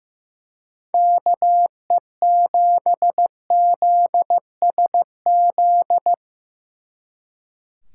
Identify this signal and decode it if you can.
Morse code, heard as a keyed tone, about 17 wpm, 700 Hz. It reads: KE7ZSZ
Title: Speed Racing - USA 2x3 Call Signs 15wpm